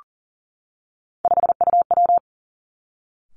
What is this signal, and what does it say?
Morse code, heard as a keyed tone, about 38 wpm, 700 Hz. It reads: 5UW
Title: Three Challenging Letter-Number Combinations 40wpm